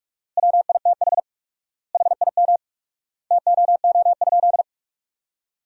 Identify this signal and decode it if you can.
Morse code, heard as a keyed tone, about 41 wpm, 700 Hz. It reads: WITH HIM TOO?
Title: 4x Word-Spacing - Sentences from Top 500 Words - 45wpm